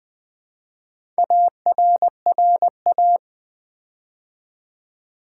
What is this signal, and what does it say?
Morse code, heard as a keyed tone, about 20 wpm, 700 Hz. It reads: ARRA
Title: Build A Word - 4 Letters At A Time - 11-Letter Words - 20wpm